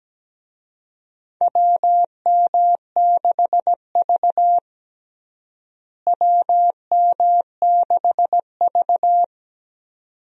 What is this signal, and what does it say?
Morse code, heard as a keyed tone, about 17 wpm, 700 Hz. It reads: WM6V WM6V
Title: USA Call Signs - 4 Characters - Repeated 2x - 17wpm